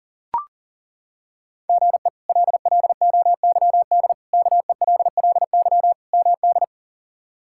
Morse code, heard as a keyed tone, about 40 wpm, 700 Hz. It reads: GE LLOYD KELLY MD
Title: K1USN SST QSO Practice - 40wpm